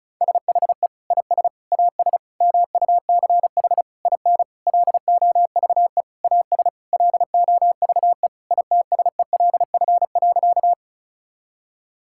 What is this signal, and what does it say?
Morse code, heard as a keyed tone, about 36 wpm, 700 Hz. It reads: SHE IS AS MUCH IN LOVE AS LOVE ITSELF.
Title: Sentences from Top 1000 Words - Review Part 5 - 35wpm